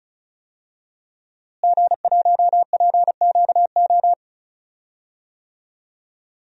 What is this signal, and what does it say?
Morse code, heard as a keyed tone, about 34 wpm, 700 Hz. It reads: G1PQO
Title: DXCC-100 Call Signs - 5 Characters - Repeated 3x - 35wpm